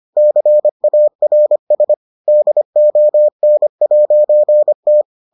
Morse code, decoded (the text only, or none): CARS DON'T